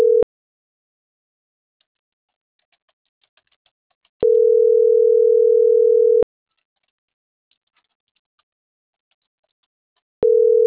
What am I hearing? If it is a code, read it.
Ringback tone (North America), 440 + 480 Hz.